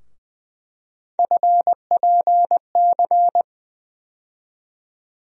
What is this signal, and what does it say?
Morse code, heard as a keyed tone, about 21 wpm, 700 Hz. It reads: FPC
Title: USA Call Sign Postfixes - 3 Letters 20wpm